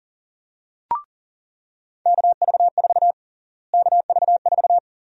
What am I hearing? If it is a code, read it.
Morse code, heard as a keyed tone, about 40 wpm, 700 Hz. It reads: KV4 KV4
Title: USA Call Sign Prefixes - Repeated 2x - 40wpm